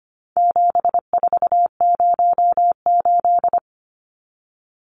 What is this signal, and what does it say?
Morse code, heard as a keyed tone, about 25 wpm, 700 Hz. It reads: 7408
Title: Numbers - 4-Digit 25wpm